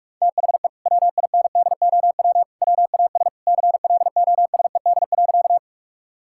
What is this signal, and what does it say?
Morse code, heard as a keyed tone, about 46 wpm, 700 Hz. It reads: THE WINDOW WAS CLOSED.
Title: Sentences from Top 1600 Words 45wpm